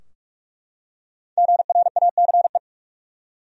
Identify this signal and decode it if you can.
Morse code, heard as a keyed tone, about 44 wpm, 700 Hz. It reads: GRACE
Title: USA Common Names - 45wpm